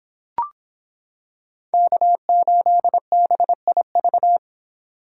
Morse code, heard as a keyed tone, about 26 wpm, 700 Hz. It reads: K8BIV